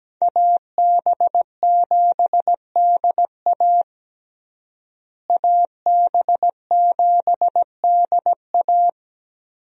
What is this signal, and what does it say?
Morse code, heard as a keyed tone, about 17 wpm, 700 Hz. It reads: AB7DA AB7DA
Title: USA Call Signs - 5 Characters - Repeated 3x - 17wpm